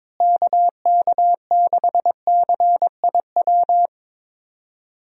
Morse code, heard as a keyed tone, about 22 wpm, 700 Hz. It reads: KK6CIW
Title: USA Call Signs - 6 Characters - Repeated 2x - 22wpm